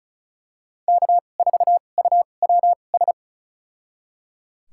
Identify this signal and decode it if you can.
Morse code, heard as a keyed tone, about 36 wpm, spaced as for 25 wpm, 700 Hz. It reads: K4UWS
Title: USA Call Signs - 5 Characters - 2x Character Spacing - 35wpm